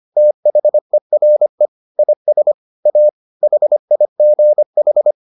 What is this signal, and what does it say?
Morse code, heard as a keyed tone, about 25 wpm, 600 Hz. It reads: THERE IS A HIGH